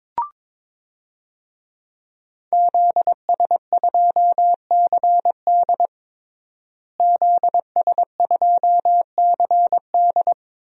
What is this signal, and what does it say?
Morse code, heard as a keyed tone, about 22 wpm, 700 Hz. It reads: ZS2CD ZS2CD
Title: DXCC-100 Call Signs - 5 Characters - Repeated 2x - 22wpm